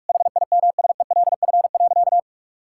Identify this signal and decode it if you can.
Morse code, heard as a keyed tone, about 46 wpm, 700 Hz. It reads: HIMSELF.
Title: Sentences from Top 1600 Words 45wpm